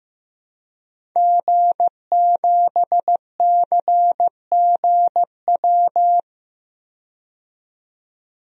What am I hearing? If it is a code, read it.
Morse code, heard as a keyed tone, about 15 wpm, 700 Hz. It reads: G7CGW
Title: DXCC-100 Call Signs - 5 Characters - 15wpm